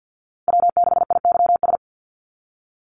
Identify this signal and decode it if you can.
Morse code, heard as a keyed tone, about 50 wpm, 700 Hz. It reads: W6IYS
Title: NA SOTA Chaser Callsigns 50wpm